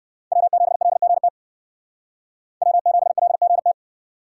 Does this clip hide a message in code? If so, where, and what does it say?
Yes: Morse code from 0.3 to 3.7 s, heard as a keyed tone, about 64 wpm, 700 Hz. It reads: W7LGT W7LGT